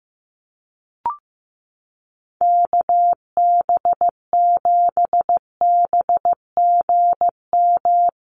Morse code, heard as a keyed tone, about 15 wpm, 700 Hz. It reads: KB7BGM
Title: USA Call Signs Repeated 3x 15wpm